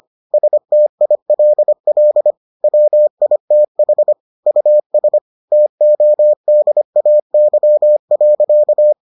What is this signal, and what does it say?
Morse code, heard as a keyed tone, about 25 wpm, 600 Hz. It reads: STILL WITH US TODAY.